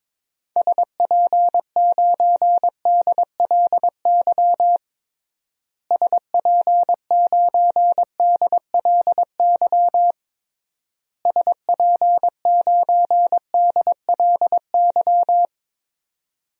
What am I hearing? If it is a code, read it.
Morse code, heard as a keyed tone, about 22 wpm, 700 Hz. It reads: SP9DLY SP9DLY SP9DLY
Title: DXCC-100 Call Signs - 6 Characters - Repeated 3x - 22wpm